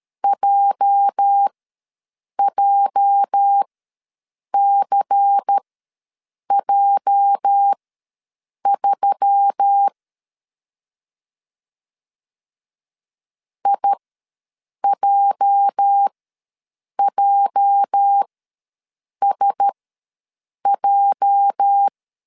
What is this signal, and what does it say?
Morse code, heard as a keyed tone, about 13 wpm, spaced as for 7 wpm, 800 Hz. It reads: JJCJ3 IJJSJ
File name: LEZIONI / 8 YJ45 / 8 YJ ws 12wpm 3342